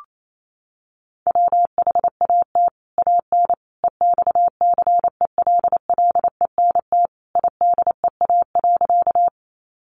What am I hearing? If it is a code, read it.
Morse code, heard as a keyed tone, about 28 wpm, 700 Hz. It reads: WHAT AN EXCELLENT IDEA.
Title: Sentences from Top 1000 Words 28wpm